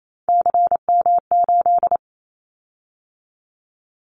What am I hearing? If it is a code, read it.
Morse code, heard as a keyed tone, about 28 wpm, 700 Hz. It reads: CM8
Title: Callsign Prefixes for Top 100 DXCC - 28wpm